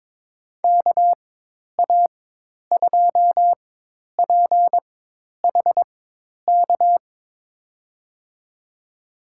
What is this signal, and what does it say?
Morse code, heard as a keyed tone, about 22 wpm, spaced as for 10 wpm, 700 Hz. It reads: KA2PHK
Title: USA Call Signs - 6 Characters - 4x Character Spacing - 22wpm